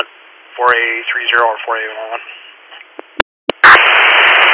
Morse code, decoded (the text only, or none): none